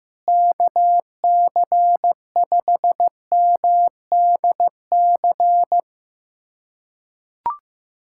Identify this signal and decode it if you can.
Morse code, heard as a keyed tone, about 15 wpm, 700 Hz. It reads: KC5MDC
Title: USA Callsign DE Callsign - 6 Characters - 15wpm